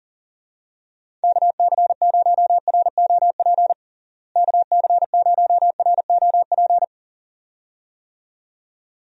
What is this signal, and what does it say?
Morse code, heard as a keyed tone, about 40 wpm, 700 Hz. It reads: KC0ROP KC0ROP
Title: USA Call Signs - 6 Characters - Repeated 2x - 40wpm